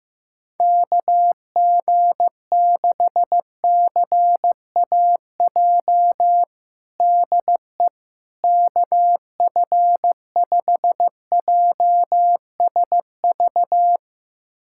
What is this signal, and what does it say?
Morse code, heard as a keyed tone, about 15 wpm, 700 Hz. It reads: KG6CAJ DE KF5JSV
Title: USA Callsign DE Callsign - 6 Characters - 15wpm